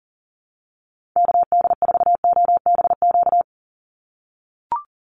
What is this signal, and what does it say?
Morse code, heard as a keyed tone, about 40 wpm, 700 Hz. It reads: KD4OBQ
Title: USA Call Signs 40wpm